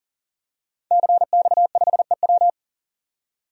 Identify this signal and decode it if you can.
Morse code, heard as a keyed tone, about 40 wpm, 700 Hz. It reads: CX5EW
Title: DXCC-100 Call Signs - 5 Characters - 40wpm